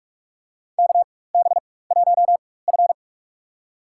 Morse code, heard as a keyed tone, about 44 wpm, spaced as for 21 wpm, 700 Hz. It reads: KB1F
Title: USA Call Signs - 4 Characters - 4x Character Spacing - 45wpm